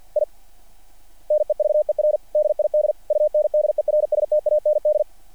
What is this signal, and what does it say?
Morse code, heard as a keyed tone, about 48 wpm, 600 Hz. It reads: I NEVER DID UNDERSTAND
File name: archived / CRSnet